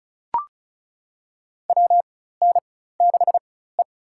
Morse code, heard as a keyed tone, about 34 wpm, spaced as for 16 wpm, 700 Hz. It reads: WN6E
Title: USA Call Signs - 4 Characters - 4x Character Spacing - 35wpm